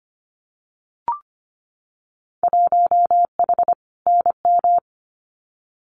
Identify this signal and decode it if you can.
Morse code, heard as a keyed tone, about 25 wpm, 700 Hz. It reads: 1H NM
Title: Winter Field Day Exchanges 25wpm